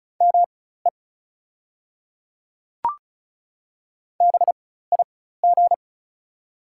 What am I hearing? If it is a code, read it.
Morse code, heard as a keyed tone, about 36 wpm, spaced as for 16 wpm, 690 Hz. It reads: ME BIG
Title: 4x Character Spacing - Top 300 Words - 35wpm